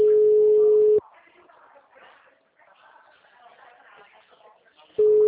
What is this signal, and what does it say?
Ringback tone (Europe, 425 Hz), with speech.